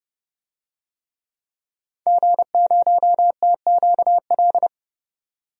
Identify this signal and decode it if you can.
Morse code, heard as a keyed tone, about 30 wpm, 700 Hz. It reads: G0TQL
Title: DXCC-100 Call Signs - 5 Characters - 30wpm